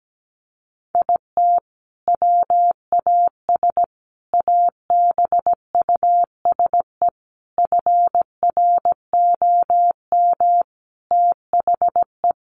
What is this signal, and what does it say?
Morse code, heard as a keyed tone, about 17 wpm, 700 Hz. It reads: IT WAS ABUSE FROM THE
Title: Sentences from Top 1300 Words 17wpm